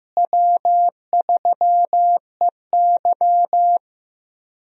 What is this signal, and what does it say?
Morse code, heard as a keyed tone, about 15 wpm, 700 Hz. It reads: W3EY
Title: USA Call Signs - 4 Characters - Repeated 2x - 15wpm